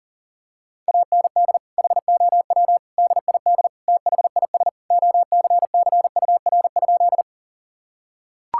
Morse code, heard as a keyed tone, about 40 wpm, 700 Hz. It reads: AND HOW DID THIS OCCUR?
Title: Sentences from Top 1000 Words - Review Part 6 - 40wpm